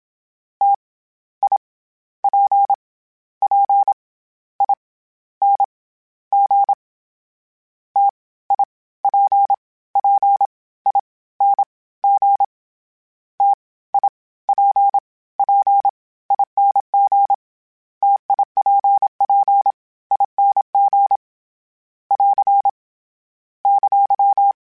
Morse code, heard as a keyed tone, about 27 wpm, spaced as for 20 wpm, 800 Hz. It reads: T I P P I N G T I P P I N G TIPPING TIPPING <AR> !
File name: Difficult Combos / Difficult Combos-mp3s / T AOEI @25